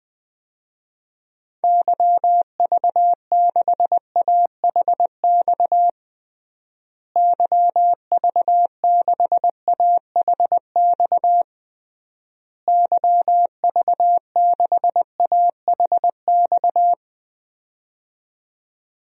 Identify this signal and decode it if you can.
Morse code, heard as a keyed tone, about 20 wpm, 700 Hz. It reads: YV6AHX YV6AHX YV6AHX
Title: DXCC-100 Call Signs - 6 Characters - Repeated 3x - 20wpm